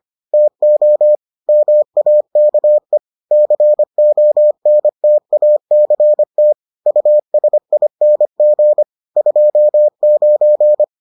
Morse code, heard as a keyed tone, about 25 wpm, 600 Hz. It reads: TO MAKE CONTACT USING 29